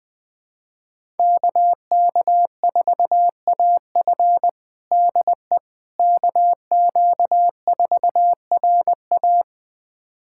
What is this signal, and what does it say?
Morse code, heard as a keyed tone, about 21 wpm, 700 Hz. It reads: KK4AF DE KQ4RA
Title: USA Callsign DE Callsign - 5 Characters - 20wpm